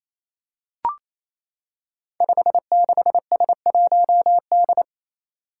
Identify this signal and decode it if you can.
Morse code, heard as a keyed tone, about 27 wpm, 700 Hz. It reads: 56S1D
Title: Five Challenging Letter-Number Combinations 28wpm